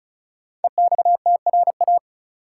Morse code, heard as a keyed tone, about 34 wpm, 700 Hz. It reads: EXTRA
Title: Sentences from Top 1000 Words - New Word Drill - 35wpm